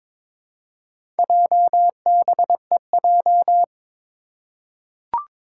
Morse code, heard as a keyed tone, about 22 wpm, 700 Hz. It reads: JBEJ
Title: Four Challenging Letter-Number Combinations 22wpm